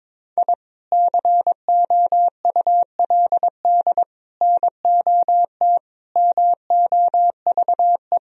Morse code, heard as a keyed tone, about 22 wpm, 700 Hz. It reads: I COULD NOT MOVE
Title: Sentences from Top 1000 Words - Review Part 1 - 22wpm